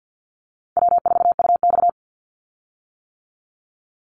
Morse code, heard as a keyed tone, about 50 wpm, 700 Hz. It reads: W4UX
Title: Easy Call Signs - 50wpm